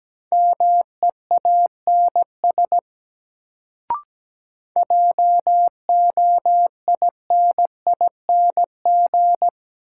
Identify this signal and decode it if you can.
Morse code, heard as a keyed tone, about 17 wpm, 700 Hz. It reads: MEANS JOINING
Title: Sets of 5 Words - New Word Drill - 17wpm